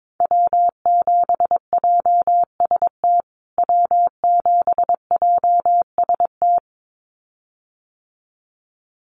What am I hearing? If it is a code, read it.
Morse code, heard as a keyed tone, about 22 wpm, 700 Hz. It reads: W7JST W7JST
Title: USA Call Signs Repeated 2x 22wpm